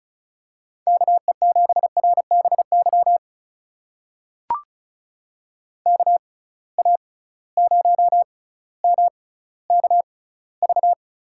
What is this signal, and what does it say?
Morse code, heard as a keyed tone, about 34 wpm, spaced as for 18 wpm, 700 Hz. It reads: KE7RBY KA0MKV
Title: USA Call Signs - 6 Characters - 6x Character Spacing - 35wpm